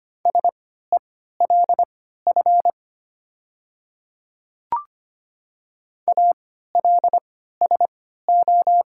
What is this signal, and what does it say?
Morse code, heard as a keyed tone, about 25 wpm, spaced as for 14 wpm, 700 Hz. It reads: SELF ALSO
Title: 3x Character Spacing - Top 200 Words - 25wpm